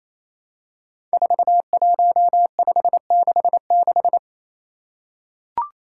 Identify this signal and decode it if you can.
Morse code, heard as a keyed tone, about 29 wpm, 700 Hz. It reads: 41566
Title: Five Challenging Letter-Number Combinations 28wpm